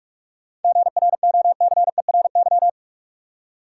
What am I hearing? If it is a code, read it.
Morse code, heard as a keyed tone, about 46 wpm, 700 Hz. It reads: GROCERY
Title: Seven Letter Words - 45wpm